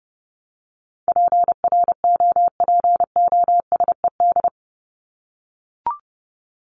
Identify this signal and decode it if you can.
Morse code, heard as a keyed tone, about 30 wpm, 700 Hz. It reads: PROPOSED
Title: ED Suffix - Top 100 Words with Base Words 30wpm